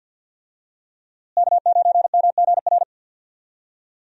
Morse code, heard as a keyed tone, about 52 wpm, 700 Hz. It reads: K9MGR
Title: USA Call Signs - 5 Characters - Repeated 3x - 50wpm